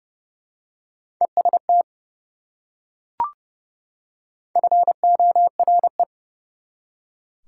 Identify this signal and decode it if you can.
Morse code, heard as a keyed tone, about 30 wpm, 700 Hz. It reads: EST FORE
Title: Prefixes and Suffixes 30wpm